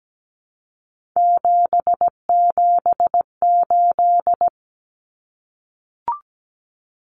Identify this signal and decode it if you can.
Morse code, heard as a keyed tone, about 17 wpm, 700 Hz. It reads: 778
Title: Numbers - 3-Digit 17wpm